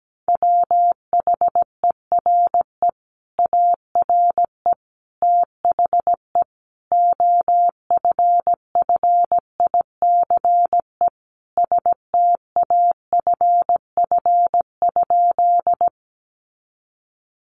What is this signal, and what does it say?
Morse code, heard as a keyed tone, about 17 wpm, 700 Hz. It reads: WHERE ARE THE OFFICE STAFF?
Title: Sentences from Top 1200 Words 17wpm